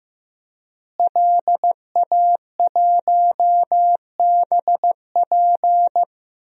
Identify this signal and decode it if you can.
Morse code, heard as a keyed tone, about 15 wpm, 700 Hz. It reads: LA1BP
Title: DXCC-100 Call Signs - 5 Characters - Repeated 3x - 15wpm